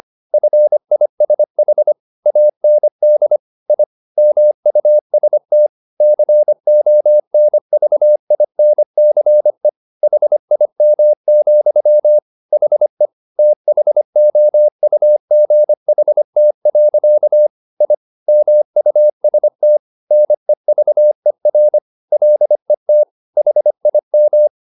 Morse code, heard as a keyed tone, about 25 wpm, 600 Hz. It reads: FISH AND I MUST CONVINCE HIM, HE THOUGHT. I MUST NEVER LET HIM
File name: the old man and the sea part 1 25wpm